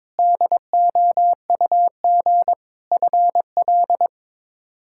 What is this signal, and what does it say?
Morse code, heard as a keyed tone, about 22 wpm, 700 Hz. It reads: DOUG FL
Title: K1USN SST QSO Practice - 22wpm